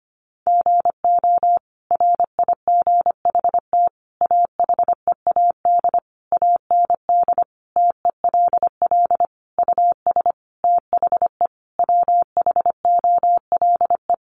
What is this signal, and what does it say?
Morse code, heard as a keyed tone, about 25 wpm, 700 Hz. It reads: GO RIGHT AHEAD AND TELL US THE WHOLE
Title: Sentences from Top 1100 Words 25wpm